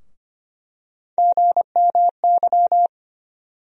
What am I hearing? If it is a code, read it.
Morse code, heard as a keyed tone, about 25 wpm, 700 Hz. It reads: GMY
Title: USA Call Sign Postfixes - 3 Letters 25wpm